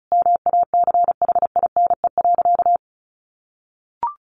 Morse code, heard as a keyed tone, about 34 wpm, 700 Hz. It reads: MACHINE.
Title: A Sound Of Thunder 35wpm